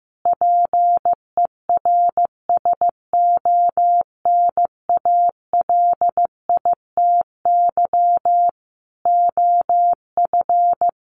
Morse code, heard as a keyed tone, about 15 wpm, 700 Hz. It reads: PERSONALITY OF
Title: Sentences from Top 1400 Words 15wpm